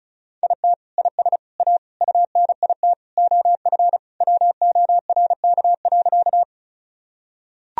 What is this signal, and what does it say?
Morse code, heard as a keyed tone, about 34 wpm, 700 Hz. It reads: IT IS A UNIT OF WORK.
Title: Sentences from Top 1000 Words - Review Part 2 - 35wpm